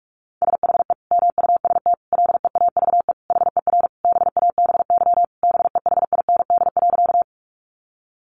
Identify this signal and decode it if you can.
Morse code, heard as a keyed tone, about 44 wpm, 700 Hz. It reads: SHE MUST LEAVE HER BABY BEHIND.
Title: Sentences from Top 1300 Words 45wpm